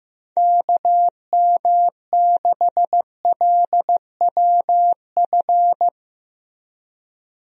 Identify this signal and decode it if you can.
Morse code, heard as a keyed tone, about 15 wpm, 700 Hz. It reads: KM6LWF
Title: USA Call Signs - 6 Characters - Repeated 2x - 15wpm